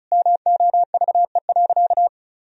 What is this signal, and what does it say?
Morse code, heard as a keyed tone, about 36 wpm, 700 Hz. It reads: MOVE.
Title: Sentences from Top 1000 Words - Review Part 1 - 35wpm